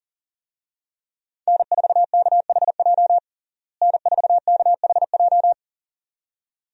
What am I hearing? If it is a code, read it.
Morse code, heard as a keyed tone, about 40 wpm, 700 Hz. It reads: N4KHJ N4KHJ
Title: USA Call Signs - 5 Characters - Repeated 2x - 40wpm